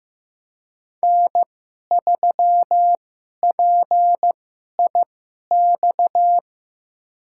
Morse code, heard as a keyed tone, about 15 wpm, spaced as for 11 wpm, 700 Hz. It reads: N3PIX
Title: USA Call Signs - 5 Characters - 2x Character Spacing - 15wpm